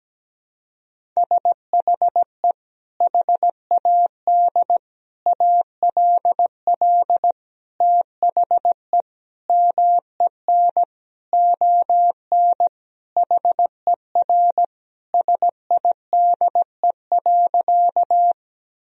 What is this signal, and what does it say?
Morse code, heard as a keyed tone, about 17 wpm, 700 Hz. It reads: SHE HAD ALL THE MEN ON HER SIDE.
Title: Sentences from Top 1000 Words - Review Part 1 - 17wpm